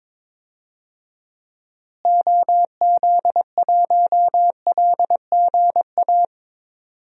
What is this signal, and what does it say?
Morse code, heard as a keyed tone, about 22 wpm, 700 Hz. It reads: OZ1LGA
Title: DXCC-100 Call Signs - 6 Characters - Repeated 2x - 22wpm